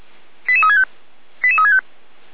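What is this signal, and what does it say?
Five-tone selective call (ZVEI), heard as quick runs of five tones, tones of about 70 ms. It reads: 88365 88365
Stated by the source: ZVEI five-tone selective call received on SDR by feuerwehr-orlamuende.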